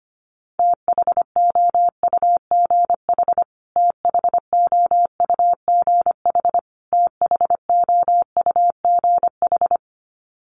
Morse code, heard as a keyed tone, about 25 wpm, 700 Hz. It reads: THOUGH THOUGH THOUGH
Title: Top 500 Words Repeated 25wpm